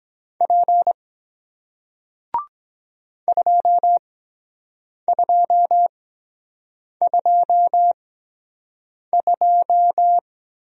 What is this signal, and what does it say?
Morse code, heard as a keyed tone, about 23 wpm, spaced as for 7 wpm, 700 Hz. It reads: P 2222